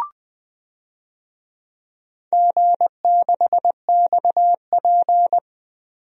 Morse code, heard as a keyed tone, about 20 wpm, 700 Hz. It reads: G6XP
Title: DXCC-100 Call Signs - 4 Characters - Repeated 3x - 20wpm